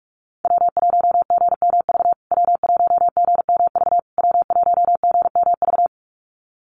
Morse code, heard as a keyed tone, about 44 wpm, 700 Hz. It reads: W1GMV W1GMV W1GMV
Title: USA Call Signs Repeated 3x 45wpm